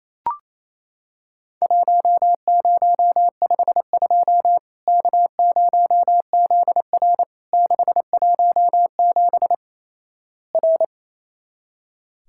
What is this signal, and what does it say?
Morse code, heard as a keyed tone, about 28 wpm, 700 Hz. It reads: 1052 K0ZR 617 R
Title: Worked All Europe DX Contest - QTC Exchange - 28wpm